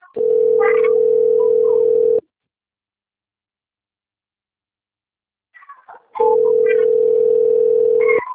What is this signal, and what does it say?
Ringback tone (North America), 440 + 480 Hz.
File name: rg-1600-unknown- - - .1358